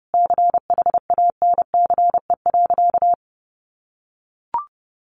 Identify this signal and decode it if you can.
Morse code, heard as a keyed tone, about 28 wpm, 700 Hz. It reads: CHANCE.
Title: Sentences from Top 1000 Words 30wpm